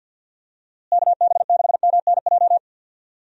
Morse code, heard as a keyed tone, about 51 wpm, 700 Hz. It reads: KB6MNJ
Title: USA Call Signs - 6 Characters - Repeated 2x - 50wpm